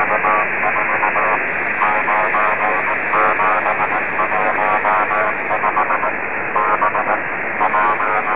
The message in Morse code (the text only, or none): none